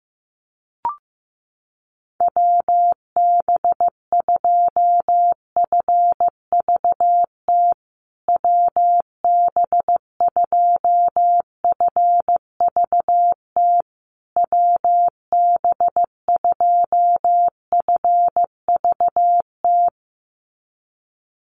Morse code, heard as a keyed tone, about 15 wpm, 700 Hz. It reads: WB2FVT WB2FVT WB2FVT
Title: USA Call Signs Repeated 3x 15wpm